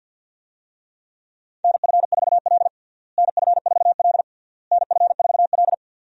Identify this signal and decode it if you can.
Morse code, heard as a keyed tone, about 50 wpm, 700 Hz. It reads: NF4L NF4L NF4L
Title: USA Call Signs - 4 Characters - Repeated 3x - 50wpm